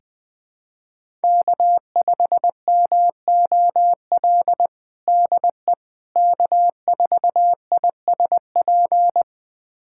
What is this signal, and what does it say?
Morse code, heard as a keyed tone, about 20 wpm, 700 Hz. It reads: K5MOL DE K4ISP
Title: USA Callsign DE Callsign - 5 Characters - 20wpm